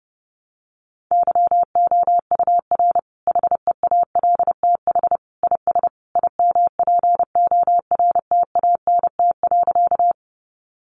Morse code, heard as a keyed tone, about 30 wpm, 700 Hz. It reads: YOUR HEALTH IS IMPORTANT.
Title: Sentences from Top 600 Words 30wpm